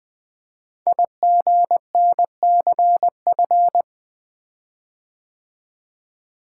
Morse code, heard as a keyed tone, about 20 wpm, 700 Hz. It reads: IGNCF
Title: Code Groups - 5 Letters - 20wpm